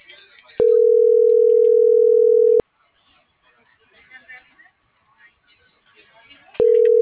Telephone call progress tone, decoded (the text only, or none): ringback tone